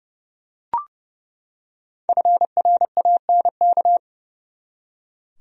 Morse code, heard as a keyed tone, about 30 wpm, 700 Hz. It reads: FRANK
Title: USA Common Names - Max 5 Letters - 30wpm